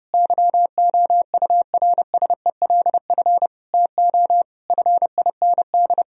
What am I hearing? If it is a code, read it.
Morse code, heard as a keyed tone, about 30 wpm, 700 Hz. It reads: YOURSELF TO FIND
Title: Sentences from Top 1000 Words - Review Part 4 - 30wpm